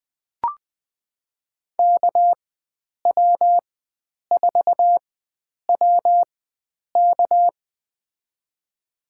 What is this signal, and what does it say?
Morse code, heard as a keyed tone, about 20 wpm, spaced as for 9 wpm, 700 Hz. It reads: KW4WK
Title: USA Call Signs - 4x Character Spacing - 20wpm